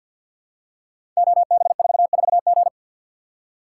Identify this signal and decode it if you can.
Morse code, heard as a keyed tone, about 47 wpm, 700 Hz. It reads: OB44G
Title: Code Groups - 5 Letter:Numbers - 50wpm